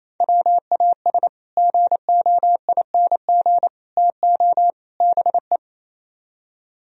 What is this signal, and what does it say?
Morse code, heard as a keyed tone, about 35 wpm, 700 Hz. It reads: WAS GOING TO BE
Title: Speed-Racing - Sets of 4 Words 28wpm